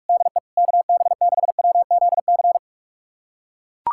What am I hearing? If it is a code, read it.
Morse code, heard as a keyed tone, about 41 wpm, 700 Hz. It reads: DE KB6WZC
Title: USA Callsign DE Callsign - 6 Characters - 45wpm